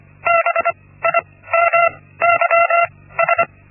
Morse code, heard as a keyed tone, about 25 wpm, spaced as for 16 wpm, 680 Hz. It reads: BIMYS
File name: Linksmakalnis-1